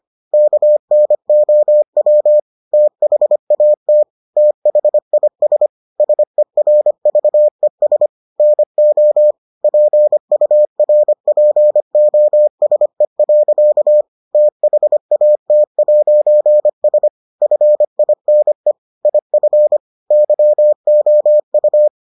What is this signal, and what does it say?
Morse code, heard as a keyed tone, about 25 wpm, 600 Hz. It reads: KNOW THAT THIS SERVES NO PURPOSE. THAT'S FINE IF YOU